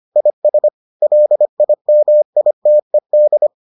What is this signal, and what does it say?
Morse code, heard as a keyed tone, about 25 wpm, 600 Hz. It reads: IS LIMITED